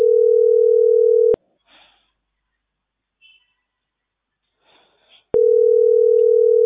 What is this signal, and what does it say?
Ringback tone (North America), 440 + 480 Hz.